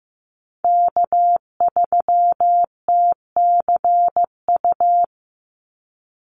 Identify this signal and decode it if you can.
Morse code, heard as a keyed tone, about 15 wpm, 700 Hz. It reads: K3TCU
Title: NA SOTA Chaser Callsigns 15wpm